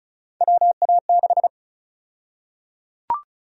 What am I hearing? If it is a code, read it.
Morse code, heard as a keyed tone, about 34 wpm, 700 Hz. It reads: WA6
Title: USA Call Sign Prefixes - 35wpm